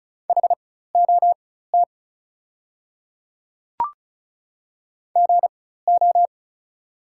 Morse code, heard as a keyed tone, about 34 wpm, spaced as for 16 wpm, 690 Hz. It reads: HOT GO
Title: 4x Character Spacing - Top 100 Words - 35wpm